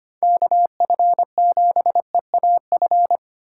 Morse code, heard as a keyed tone, about 25 wpm, 700 Hz. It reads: KF7EAF
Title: USA Call Signs - 4x Character Spacing - 25wpm